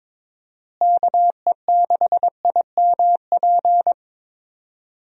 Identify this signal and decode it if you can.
Morse code, heard as a keyed tone, about 22 wpm, 700 Hz. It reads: KE6IMP
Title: USA Call Signs - 6 Characters - Repeated 3x - 22wpm